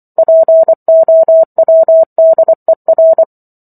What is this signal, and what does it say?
Morse code, heard as a keyed tone, about 24 wpm, 650 Hz. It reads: POWDER
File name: Chapter 15 - 24 WPM